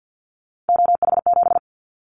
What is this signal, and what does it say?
Morse code, heard as a keyed tone, about 50 wpm, 700 Hz. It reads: OH7
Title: Callsign Prefixes for Top 25 DXCC - 50wpm